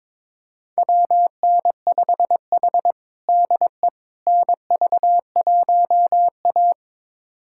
Morse code, heard as a keyed tone, about 22 wpm, 700 Hz. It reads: WN5H DE NV1A
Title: USA Callsign DE Callsign - 4 Characters - 22wpm